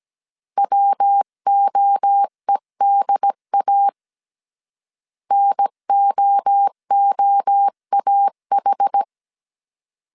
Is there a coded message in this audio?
Yes: Morse code from 0.6 to 9.0 s, heard as a keyed tone, about 17 wpm, 800 Hz. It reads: WOEDA NOOAH